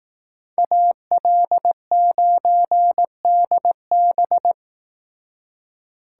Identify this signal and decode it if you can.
Morse code, heard as a keyed tone, about 18 wpm, 700 Hz. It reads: AL9DB
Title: Speed Racing - USA Call Signs 15wpm